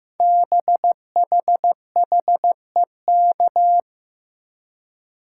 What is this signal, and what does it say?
Morse code, heard as a keyed tone, about 15 wpm, 700 Hz. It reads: BHHEK
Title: Code Groups - 5 Letters - 15wpm